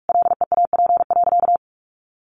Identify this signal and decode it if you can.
Morse code, heard as a keyed tone, about 46 wpm, 700 Hz. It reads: LEAP.
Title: A Sound Of Thunder 45wpm